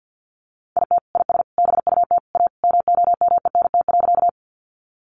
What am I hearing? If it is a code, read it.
Morse code, heard as a keyed tone, about 50 wpm, 700 Hz. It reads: IT IS BUT A MOMENT.
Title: Sentences from Top 1000 Words 50wpm